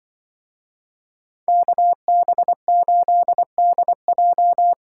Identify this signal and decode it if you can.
Morse code, heard as a keyed tone, about 24 wpm, 700 Hz. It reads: KB8DJ
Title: Speed Racing - USA 2x2 Call Signs 20wpm